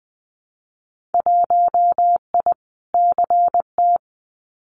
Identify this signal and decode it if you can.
Morse code, heard as a keyed tone, about 20 wpm, 700 Hz. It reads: 1I CT
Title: Winter Field Day Exchanges 20wpm